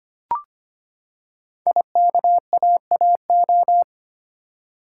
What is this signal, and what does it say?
Morse code, heard as a keyed tone, about 25 wpm, 700 Hz. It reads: IKAAO
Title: Code Groups - 5 Letters - 25wpm